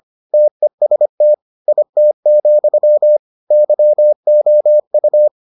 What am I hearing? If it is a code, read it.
Morse code, heard as a keyed tone, about 25 wpm, 600 Hz. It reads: TEST IT, YOU